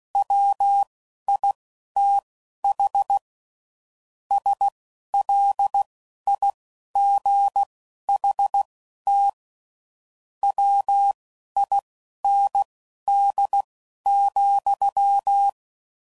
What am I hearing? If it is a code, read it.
Morse code, heard as a keyed tone, about 16 wpm, spaced as for 11 wpm, 800 Hz. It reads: WITH SLIGHT WIND,
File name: CW Code Groups @10WPM/Weather Sets